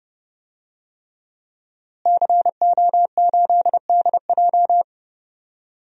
Morse code, heard as a keyed tone, about 30 wpm, 700 Hz. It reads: CO8DJ
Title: DXCC-100 Call Signs - 5 Characters - Repeated 3x - 30wpm